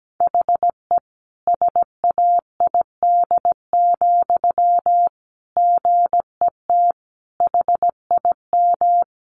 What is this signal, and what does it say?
Morse code, heard as a keyed tone, about 17 wpm, 700 Hz. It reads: HE SAID, GET HIM
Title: Sentences from Top 400 Words 17wpm